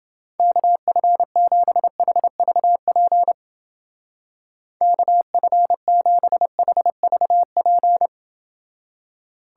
Morse code, heard as a keyed tone, about 28 wpm, 700 Hz. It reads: KF7HVP KF7HVP